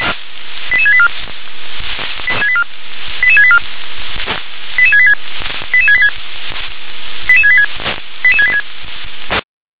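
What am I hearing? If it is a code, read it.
Five-tone selective call (ZVEI), heard as quick runs of five tones, tones of about 70 ms. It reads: 88684 88684 88686 88686 88686 88686
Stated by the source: ZVEI five-tone selective call 88770 received via SDR.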